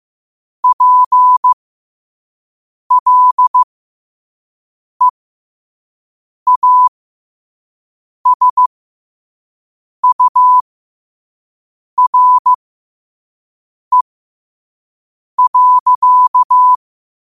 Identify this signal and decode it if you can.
Morse code, heard as a keyed tone, about 15 wpm, spaced as for 5 wpm, 1000 Hz. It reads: PLEASURE.